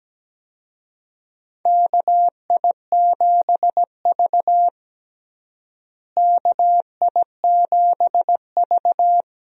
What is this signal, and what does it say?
Morse code, heard as a keyed tone, about 17 wpm, 700 Hz. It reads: KI7V KI7V
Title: USA Call Signs - 4 Characters - Repeated 2x - 17wpm